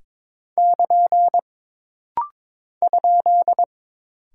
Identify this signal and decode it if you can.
Morse code, heard as a keyed tone, about 22 wpm, 700 Hz. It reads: <KN> ?
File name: practice / 22wpm / Single Punctuation-Prosign - Rapid-Fire - 22wpm